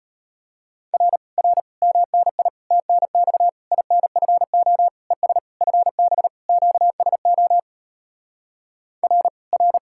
Morse code, heard as a keyed tone, about 37 wpm, 700 Hz. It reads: R R MNI TNX INFO ES FB QSO R R